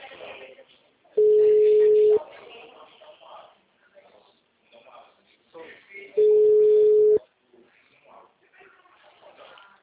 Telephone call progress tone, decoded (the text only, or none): ringback tone